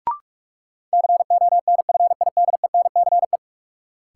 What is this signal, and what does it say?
Morse code, heard as a keyed tone, about 44 wpm, 700 Hz. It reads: CONFIDENCE
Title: Sentences from Top 1500 Words - New Word Drill - 45wpm